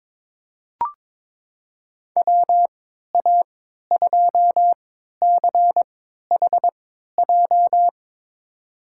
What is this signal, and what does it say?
Morse code, heard as a keyed tone, about 21 wpm, spaced as for 12 wpm, 700 Hz. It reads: WA2CHJ
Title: USA Call Signs - 6 Characters - 3x Character Spacing - 22wpm